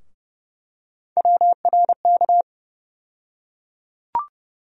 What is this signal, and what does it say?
Morse code, heard as a keyed tone, about 30 wpm, 700 Hz. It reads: WRK
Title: USA Call Sign Postfixes - 3 Letters 30wpm